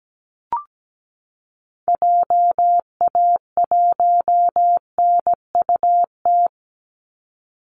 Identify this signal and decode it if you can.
Morse code, heard as a keyed tone, about 17 wpm, 700 Hz. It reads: JA1NUT
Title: Call Signs Repeated 2x - 17wpm